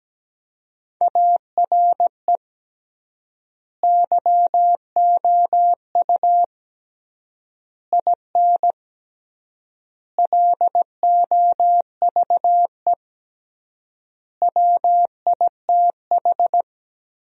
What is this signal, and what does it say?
Morse code, heard as a keyed tone, about 17 wpm, 700 Hz. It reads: ARE YOU IN LOVE WITH
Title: 3x Word-Spacing - Sentences from Top 400 Words - 17wpm